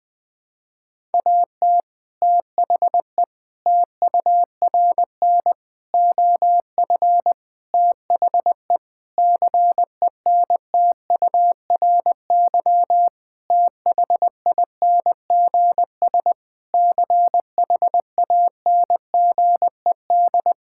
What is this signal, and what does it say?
Morse code, heard as a keyed tone, about 20 wpm, 700 Hz. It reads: AT THE TURN OF THE CENTURY THINGS CHANGED
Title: Sentences with 5-Word Phrases - 20wpm